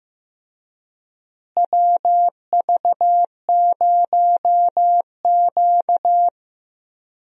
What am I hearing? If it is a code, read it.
Morse code, heard as a keyed tone, about 15 wpm, 700 Hz. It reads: WV0Q
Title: USA Call Signs - 4 Characters - Repeated 3x - 15wpm